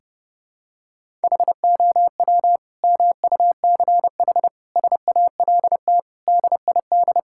Morse code, heard as a keyed tone, about 30 wpm, 700 Hz. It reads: HOW MUCH SALT DID
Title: Sentences from Top 1000 Words - Review Part 5 - 30wpm